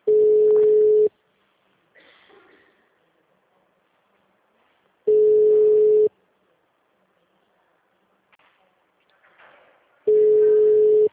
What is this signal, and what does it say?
Ringback tone (Europe, 425 Hz).
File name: out- -2004